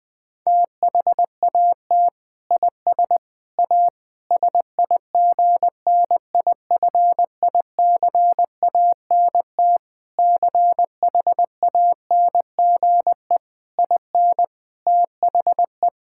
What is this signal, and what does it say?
Morse code, heard as a keyed tone, about 20 wpm, 700 Hz. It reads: THAT IS A SIGNIFICANT CHANGE IN THE
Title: Sentences from Top 1000 Words - Review Part 6 - 20wpm